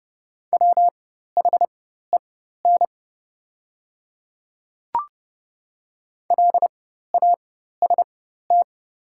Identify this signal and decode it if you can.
Morse code, heard as a keyed tone, about 30 wpm, spaced as for 14 wpm, 700 Hz. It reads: WHEN LAST
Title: 4x Character Spacing - Top 500 Words - 30wpm